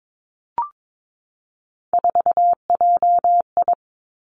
Morse code, heard as a keyed tone, about 22 wpm, 700 Hz. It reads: 4JI
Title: Three Challenging Letter-Number Combinations 22wpm